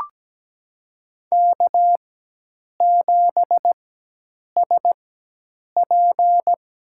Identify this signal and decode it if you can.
Morse code, heard as a keyed tone, about 17 wpm, spaced as for 8 wpm, 700 Hz. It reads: K7SP
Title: USA Call Signs - 5 Characters - 4x Character Spacing - 17wpm